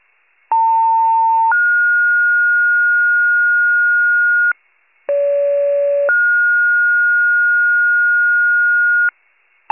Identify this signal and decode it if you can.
Two-tone page, heard as a one-second tone, then a three-second tone, about 900 then 1430 Hz, about 570 then 1430 Hz. It is heard from 0.5 to 9.1 s.
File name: pager / 78 FIRE / wcalert